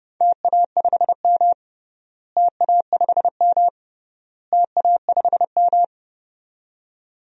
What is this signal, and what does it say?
Morse code, heard as a keyed tone, about 30 wpm, 700 Hz. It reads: TA5M TA5M TA5M
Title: DXCC-100 Call Signs - 4 Characters - Repeated 3x - 30wpm